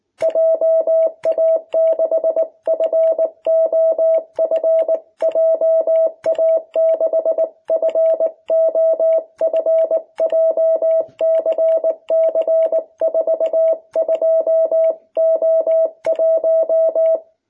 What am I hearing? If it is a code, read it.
Morse code, heard as a keyed tone, about 19 wpm, 660 Hz. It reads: JA6FOF JA6FOF JCC42O1
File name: jh8cbh / 2021cw